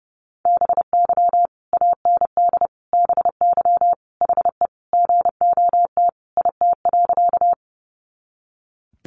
Morse code, heard as a keyed tone, about 30 wpm, 700 Hz. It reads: BY AND BY HE GOT IT.
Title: Sentences from Top 300 Words 30wpm